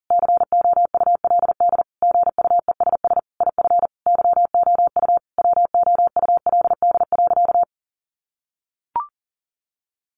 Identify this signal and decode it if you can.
Morse code, heard as a keyed tone, about 40 wpm, 700 Hz. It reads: COULD GUESS IF YOU WOULD.
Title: Sentences from Top 1500 Words 40wpm